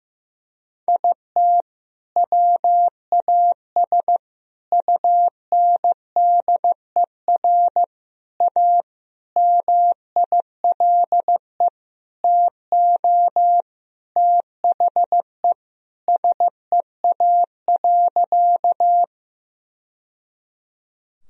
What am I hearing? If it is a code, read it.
Morse code, heard as a keyed tone, about 15 wpm, 700 Hz. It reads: IT WAS UNDER A MILE TO THE SEA.
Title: Sentences from Top 1000 Words - Review Part 1 - 15wpm